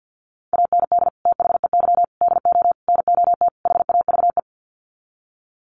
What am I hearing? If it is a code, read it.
Morse code, heard as a keyed tone, about 48 wpm, 700 Hz. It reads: AND THEY DO NOT HAVE
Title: Sets of 5 Words Encore2 50wpm